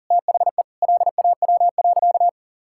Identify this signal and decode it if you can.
Morse code, heard as a keyed tone, about 40 wpm, 700 Hz. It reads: THE LAW.
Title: Sentences from Top 1000 Words - Review Part 6 - 40wpm